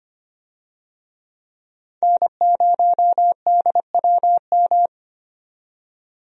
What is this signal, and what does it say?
Morse code, heard as a keyed tone, about 25 wpm, 700 Hz. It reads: N0DWM
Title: USA Call Signs - 5 Characters - 25wpm